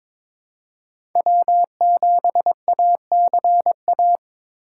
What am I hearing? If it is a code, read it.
Morse code, heard as a keyed tone, about 22 wpm, 700 Hz. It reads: W7ACA
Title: USA Call Signs - 2x Character Spacing - 22wpm